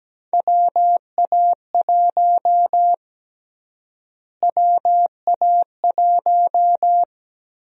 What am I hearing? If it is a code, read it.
Morse code, heard as a keyed tone, about 17 wpm, 700 Hz. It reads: WA1 WA1
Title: USA Call Sign Prefixes - Repeated 2x - 17wpm